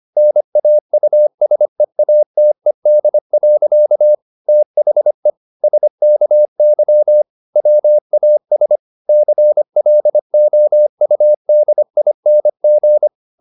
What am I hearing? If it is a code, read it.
Morse code, heard as a keyed tone, about 25 wpm, 600 Hz. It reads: NAUSEATED. THE SKY WAS CLOUDING